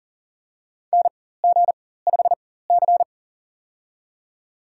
Morse code, heard as a keyed tone, about 40 wpm, spaced as for 19 wpm, 700 Hz. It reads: NG5C
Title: USA Call Signs - 4 Characters - 4x Character Spacing - 40wpm